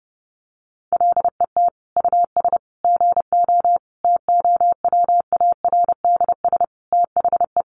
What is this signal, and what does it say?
Morse code, heard as a keyed tone, about 30 wpm, 700 Hz. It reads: LET US GO TOWARDS THE
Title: Sentences from Top 1100 Words 30wpm